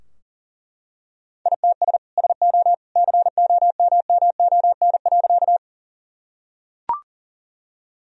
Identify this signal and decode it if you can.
Morse code, heard as a keyed tone, about 40 wpm, 700 Hz. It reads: ITS SO COMMON.
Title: Sentences from Top 1000 Words - Review Part 3 - 40wpm